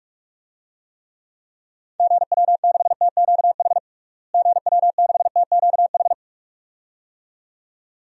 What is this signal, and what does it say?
Morse code, heard as a keyed tone, about 46 wpm, 700 Hz. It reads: GW6TQH GW6TQH
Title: DXCC-100 Call Signs - 6 Characters - Repeated 2x - 45wpm